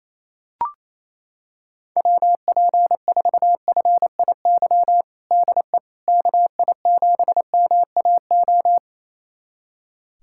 Morse code, heard as a keyed tone, about 28 wpm, 700 Hz. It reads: WP4FIY DE KI7MAO
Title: USA Callsign DE Callsign 28wpm